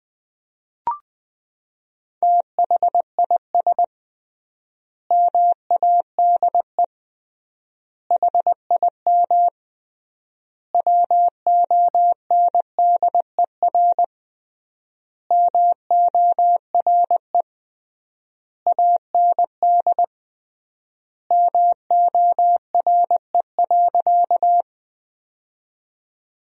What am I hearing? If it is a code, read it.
Morse code, heard as a keyed tone, about 20 wpm, 700 Hz. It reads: THIS MADE HIM WONDER MORE AND MORE.
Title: 3x Word-Spacing - Sentences from Top 500 Words - 20wpm